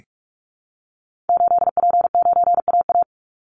Morse code, heard as a keyed tone, about 44 wpm, 700 Hz. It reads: 8P9AA
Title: Call Signs Repeated 3x - 45wpm